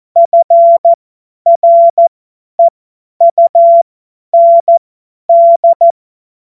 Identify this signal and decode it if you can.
Morse code, heard as a keyed tone, about 14 wpm, spaced as for 10 wpm, 680 Hz. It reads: FREUND
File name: Morsekurs / Witze / Witz 01. Februar 2022 50 F70